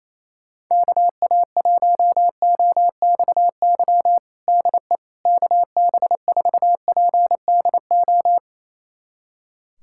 Morse code, heard as a keyed tone, about 28 wpm, 700 Hz. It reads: KA1OXY DE KB4PDO
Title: USA Callsign DE Callsign - 6 Characters - 28wpm